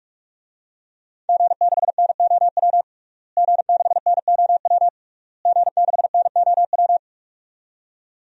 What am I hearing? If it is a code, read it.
Morse code, heard as a keyed tone, about 46 wpm, 700 Hz. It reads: G6NOW G6NOW G6NOW
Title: DXCC-100 Call Signs - 5 Characters - Repeated 3x - 45wpm